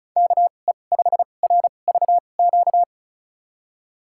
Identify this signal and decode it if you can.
Morse code, heard as a keyed tone, about 34 wpm, spaced as for 25 wpm, 700 Hz. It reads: KE5RVQ
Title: USA Call Signs - 6 Characters - 2x Character Spacing - 35wpm